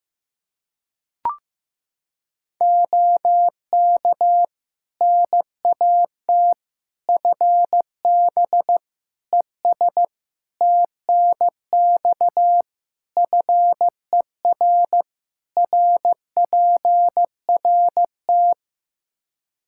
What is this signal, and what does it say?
Morse code, heard as a keyed tone, about 15 wpm, 700 Hz. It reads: OK NAT FB ES TNX FER RPRT
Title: Practice QSO - Exchange 2 - Info - 15wpm